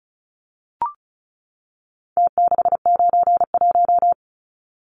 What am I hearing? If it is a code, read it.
Morse code, heard as a keyed tone, about 34 wpm, 700 Hz. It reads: T691
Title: Numbers - Contest Style - 4-Digit 35wpm